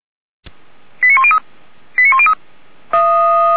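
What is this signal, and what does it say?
Five-tone selective call (ZVEI), heard as quick runs of five tones, tones of about 80 ms. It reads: 89103 89103
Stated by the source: Received by SDR.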